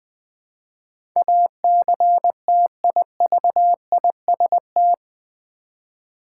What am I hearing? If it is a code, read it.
Morse code, heard as a keyed tone, about 20 wpm, 700 Hz. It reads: ACTIVIST
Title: 8 Letter Words - First 3 Letters Given - 20wpm